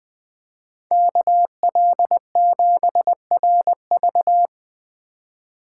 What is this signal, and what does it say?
Morse code, heard as a keyed tone, about 20 wpm, 700 Hz. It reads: KL7RV
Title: USA Call Signs - 5 Characters - 3x Character Spacing - 20wpm